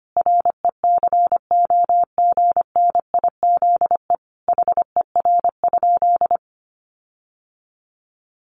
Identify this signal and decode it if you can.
Morse code, heard as a keyed tone, about 25 wpm, 700 Hz. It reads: RECOGNIZE HER?
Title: Sentences from Top 1400 Words 25wpm